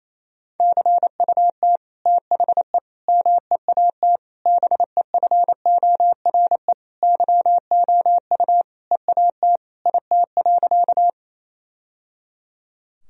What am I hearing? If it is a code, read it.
Morse code, heard as a keyed tone, about 27 wpm, 700 Hz. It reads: CUT THE MEAT BEFORE YOU EAT IT.
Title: Sentences from Top 1000 Words - Review Part 3 - 28wpm